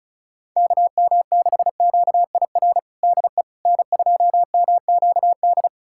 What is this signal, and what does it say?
Morse code, heard as a keyed tone, about 34 wpm, 690 Hz. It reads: KM6QIR DE N2MQD
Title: USA Callsign DE Callsign 35wpm